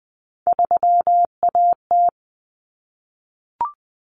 Morse code, heard as a keyed tone, about 20 wpm, 700 Hz. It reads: 3AT
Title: Common Cut Numbers - 3-Digit 20wpm